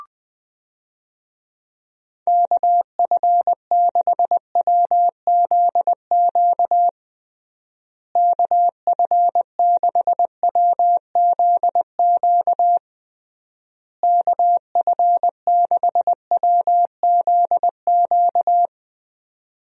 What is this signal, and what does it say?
Morse code, heard as a keyed tone, about 20 wpm, 700 Hz. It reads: KF6WZQ KF6WZQ KF6WZQ
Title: USA Call Signs - 6 Characters - Repeated 3x - 20wpm